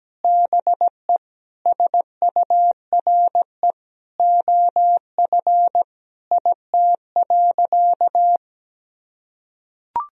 Morse code, heard as a keyed tone, about 17 wpm, 700 Hz. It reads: BE SURE OF IT.
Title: Sentences from Top 1000 Words - Review Part 4 - 17wpm